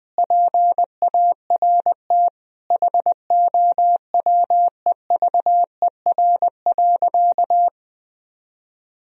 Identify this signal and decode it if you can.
Morse code, heard as a keyed tone, about 20 wpm, 700 Hz. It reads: PART HOWEVER.
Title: Sentences from Top 1000 Words - Review Part 1 - 20wpm